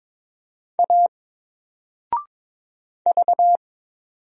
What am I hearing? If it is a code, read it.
Morse code, heard as a keyed tone, about 22 wpm, 700 Hz. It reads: A V
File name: Single-Letter-Number-22wpm